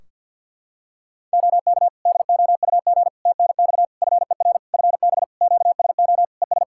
Morse code, heard as a keyed tone, about 48 wpm, 700 Hz. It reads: OK DOUG TNX FER FB QSO ES
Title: K1USN SST QSO Practice 50wpm